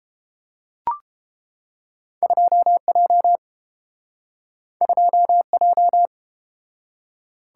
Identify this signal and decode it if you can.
Morse code, heard as a keyed tone, about 29 wpm, 700 Hz. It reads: 2J 2J